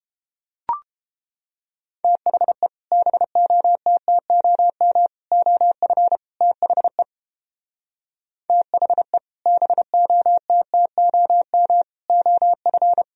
Morse code, heard as a keyed tone, about 31 wpm, 700 Hz. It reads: THE BOTTOM OF THE THE BOTTOM OF